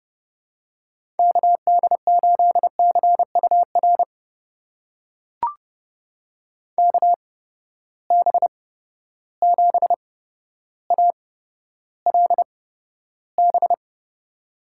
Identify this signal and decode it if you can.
Morse code, heard as a keyed tone, about 32 wpm, spaced as for 13 wpm, 700 Hz. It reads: KD8CUR KB7ALB
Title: USA Call Signs - 6 Characters - 8x Character Spacing - 30wpm